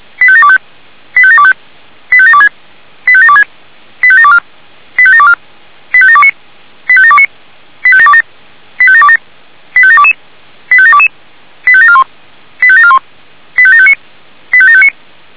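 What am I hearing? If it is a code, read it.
Five-tone selective call (ZVEI), heard as quick runs of five tones, tones of about 70 ms. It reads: 85725 85725 85726 85726 85724 85724 85729 85729 85727 85727 85722 85722 85741 85741 85759 85759
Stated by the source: ZVEI five-tone call 85735 received on an SDR.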